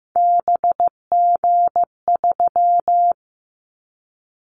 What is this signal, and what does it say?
Morse code, heard as a keyed tone, about 15 wpm, 700 Hz. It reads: BG3
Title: Callsign Prefixes for Top 100 DXCC - 15wpm